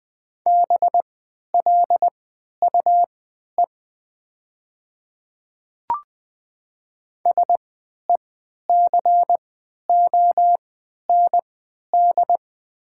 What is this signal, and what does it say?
Morse code, heard as a keyed tone, about 20 wpm, spaced as for 11 wpm, 700 Hz. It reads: BLUE SECOND
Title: 3x Character Spacing - Top 500 Words - 20wpm